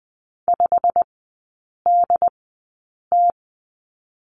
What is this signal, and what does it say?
Morse code, heard as a keyed tone, about 20 wpm, spaced as for 8 wpm, 700 Hz. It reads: 5DT
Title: 138 Lesson 23e - Letter f - Three Characters 20wpm